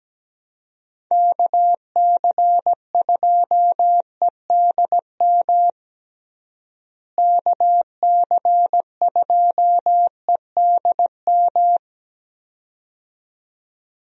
Morse code, heard as a keyed tone, about 17 wpm, 700 Hz. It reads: KC2EDM KC2EDM
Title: USA Call Signs - 6 Characters - Repeated 2x - 17wpm